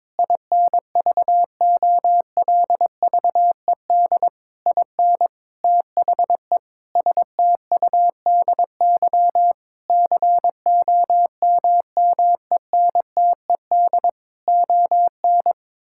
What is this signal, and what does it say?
Morse code, heard as a keyed tone, about 22 wpm, 700 Hz. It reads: INVOLVED IN THE STUDY COMMENTED ON
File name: Sentences with 7-Word Phrases - 22wpm